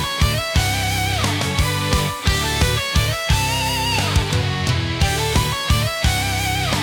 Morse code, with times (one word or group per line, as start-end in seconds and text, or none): none